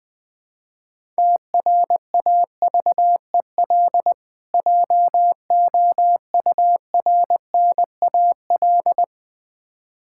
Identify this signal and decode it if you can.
Morse code, heard as a keyed tone, about 20 wpm, 700 Hz. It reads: TRAVEL JOURNAL
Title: Difficult Words with Hints - 20wpm